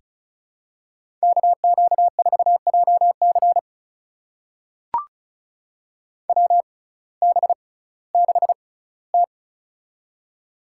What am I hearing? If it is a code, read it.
Morse code, heard as a keyed tone, about 36 wpm, spaced as for 19 wpm, 700 Hz. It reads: KQ4JC WB6T
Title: USA Call Signs - 6x Character Spacing - 35wpm